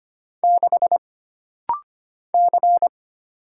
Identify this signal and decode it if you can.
Morse code, heard as a keyed tone, about 25 wpm, 700 Hz. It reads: 6 C
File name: Single Character (Letter:Number:Punctuation:Prosign) - Rapid-Fire - 25wpm